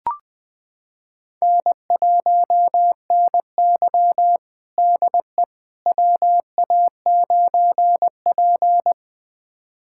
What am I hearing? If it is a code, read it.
Morse code, heard as a keyed tone, about 20 wpm, 700 Hz. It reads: N1NY DE WA9P
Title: USA Callsign DE Callsign - 4 Characters - 20wpm